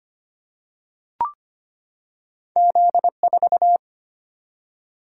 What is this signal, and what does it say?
Morse code, heard as a keyed tone, about 25 wpm, 700 Hz. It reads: Z4
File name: Two Letter-Number Combinations - 25wpm